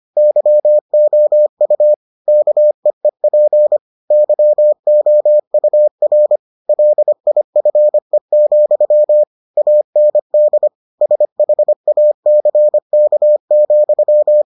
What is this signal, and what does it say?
Morse code, heard as a keyed tone, about 25 wpm, 600 Hz. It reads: YOU KEEP YOUR LIFE, AND SHACK,